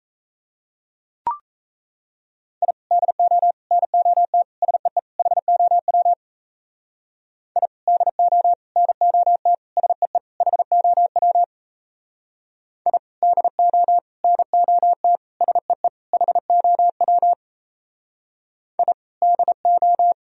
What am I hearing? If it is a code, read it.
Morse code, heard as a keyed tone, about 36 wpm, 700 Hz. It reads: I DO NOT SEE HOW I DO NOT SEE HOW I DO NOT SEE HOW I DO